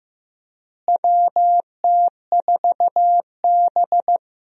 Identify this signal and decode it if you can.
Morse code, heard as a keyed tone, about 15 wpm, 700 Hz. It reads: WT4B
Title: USA Call Signs - 4 Characters - Repeated 3x - 15wpm